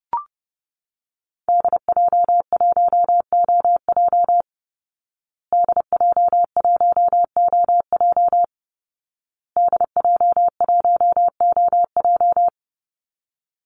Morse code, heard as a keyed tone, about 30 wpm, 700 Hz. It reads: DJ1OJ DJ1OJ DJ1OJ
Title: Call Signs Repeated 2x - 30wpm